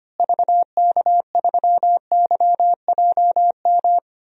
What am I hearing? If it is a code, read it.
Morse code, heard as a keyed tone, about 25 wpm, 700 Hz. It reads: VK3YJM
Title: DXCC-100 Call Signs - 6 Characters - Repeated 2x - 25wpm